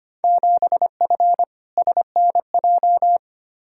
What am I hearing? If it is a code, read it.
Morse code, heard as a keyed tone, about 25 wpm, 700 Hz. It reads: 7F SNJ
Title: Field Day Exchanges Encore - 25wpm